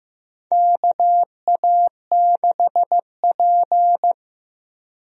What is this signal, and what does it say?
Morse code, heard as a keyed tone, about 15 wpm, 700 Hz. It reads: KA6P
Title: USA Call Signs - 4 Characters - Repeated 2x - 15wpm